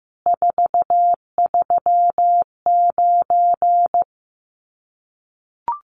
Morse code, heard as a keyed tone, about 15 wpm, 700 Hz. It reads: 439
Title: Common QSO Phrases 15wpm